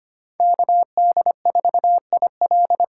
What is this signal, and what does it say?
Morse code, heard as a keyed tone, about 25 wpm, 700 Hz. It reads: KD4IL
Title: USA Call Signs - 6 Characters - 8x Character Spacing - 25wpm